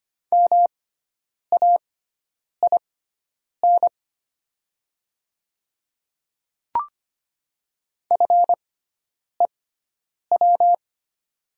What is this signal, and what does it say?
Morse code, heard as a keyed tone, about 25 wpm, spaced as for 9 wpm, 700 Hz. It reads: MAIN FEW
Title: 6x Character Spacing - Top 500 Words - 25wpm